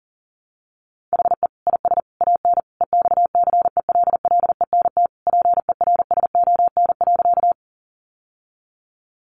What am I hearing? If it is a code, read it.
Morse code, heard as a keyed tone, about 40 wpm, 700 Hz. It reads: HE IS AN EXCELLENT PERSON.
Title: Sentences from Top 1000 Words 40wpm